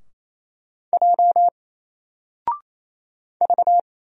Morse code, heard as a keyed tone, about 28 wpm, 700 Hz. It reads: J V
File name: Single-Letter-Number-28wpm